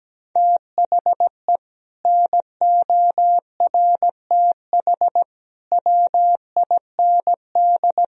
Morse code, heard as a keyed tone, about 17 wpm, 700 Hz. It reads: THE NORTH WIND
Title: Sentences from Top 1000 Words - Review Part 3 - 17wpm